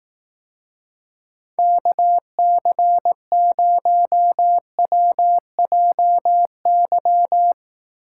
Morse code, heard as a keyed tone, about 18 wpm, 700 Hz. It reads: KC0WJY
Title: Speed Racing - USA 2x3 Call Signs 15wpm